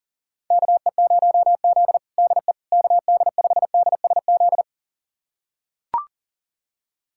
Morse code, heard as a keyed tone, about 40 wpm, 700 Hz. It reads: KE0Z DE KD5DSZ
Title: USA Callsign DE Callsign 40wpm